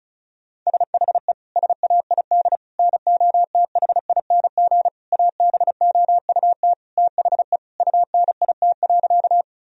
Morse code, heard as a keyed tone, about 34 wpm, 700 Hz. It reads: SHE SAID NOTHING ABOUT THE UNIT.
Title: Sentences from Top 1000 Words - Review Part 2 - 35wpm